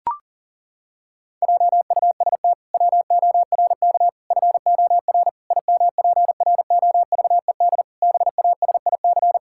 Morse code, heard as a keyed tone, about 40 wpm, 700 Hz. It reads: JUST WORK FOR IMPROVED BASIC
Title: Pangram Challenge v2 - 40wpm